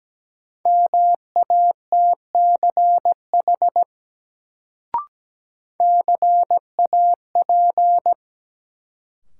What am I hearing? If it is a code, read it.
Morse code, heard as a keyed tone, about 17 wpm, 700 Hz. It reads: MATCH CAP
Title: Sentences from Top 1600 Words - New Word Drill Alt - 17wpm